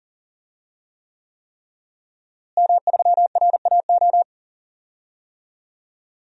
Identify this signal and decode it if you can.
Morse code, heard as a keyed tone, about 40 wpm, 700 Hz. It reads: M3RAO
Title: DXCC-100 Call Signs - 5 Characters - Repeated 3x - 40wpm